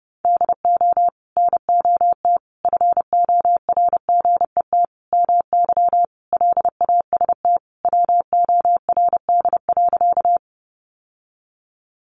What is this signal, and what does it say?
Morse code, heard as a keyed tone, about 30 wpm, 700 Hz. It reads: DO NOT FORGET MY LAST WORD.
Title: Sentences from Top 1200 Words 30wpm